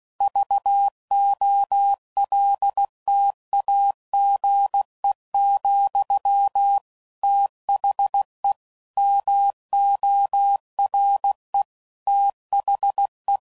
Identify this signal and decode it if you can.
Morse code, heard as a keyed tone, about 16 wpm, 800 Hz. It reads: VOLTAGE, THE MORE THE
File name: CW Code Groups @15WPM / Technical Subjects / XC